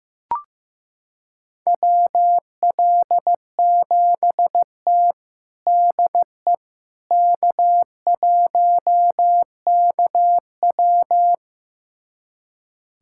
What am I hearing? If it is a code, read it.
Morse code, heard as a keyed tone, about 15 wpm, 700 Hz. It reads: WL7T DE K1KW
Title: USA Callsign DE Callsign - 4 Characters - 15wpm